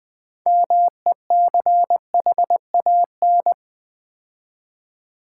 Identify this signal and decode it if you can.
Morse code, heard as a keyed tone, about 20 wpm, 700 Hz. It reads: MECHAN
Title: Build A Word - 3 Letters At A Time - 9-Letter Words - 20wpm